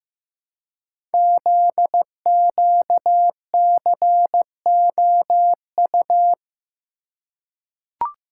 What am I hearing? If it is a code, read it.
Morse code, heard as a keyed tone, about 15 wpm, 700 Hz. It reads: ZQCOU
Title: Code Groups - 5 Letters - 15wpm